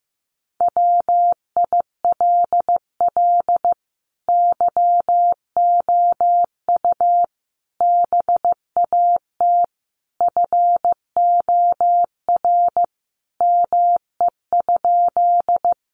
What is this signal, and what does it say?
Morse code, heard as a keyed tone, about 15 wpm, 700 Hz. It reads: WILL YOU BAT FOR ME?
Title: Sentences from Top 1500 Words 15wpm